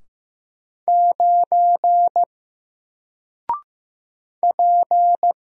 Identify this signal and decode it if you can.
Morse code, heard as a keyed tone, about 15 wpm, 700 Hz. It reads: 9 P
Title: Single Character (Letter:Number:Punctuation:Prosign) 15wpm